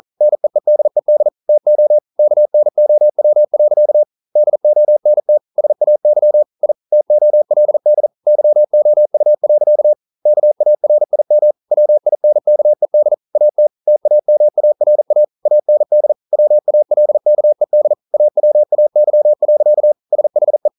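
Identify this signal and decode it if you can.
Morse code, heard as a keyed tone, about 42 wpm, 600 Hz. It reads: NEEDED TO KNOW. DONT SAY I TOLD YOU. KARIM WINKED AT TAMARA AND WALKED AWAY. SHE
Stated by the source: Morse code practice at 41 wpm, from CW med Gnister.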